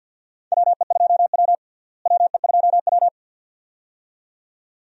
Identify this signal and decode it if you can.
Morse code, heard as a keyed tone, about 46 wpm, 700 Hz. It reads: WE2W WE2W
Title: USA Call Signs - 4 Characters - Repeated 2x - 50wpm